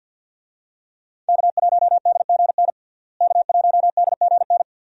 Morse code, heard as a keyed tone, about 50 wpm, 700 Hz. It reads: K1DGN K1DGN
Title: USA Call Signs - 5 Characters - Repeated 2x - 50wpm